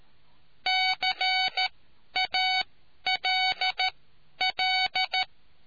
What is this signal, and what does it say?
Morse code, heard as a keyed tone, about 13 wpm, spaced as for 10 wpm, 750 Hz. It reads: CALL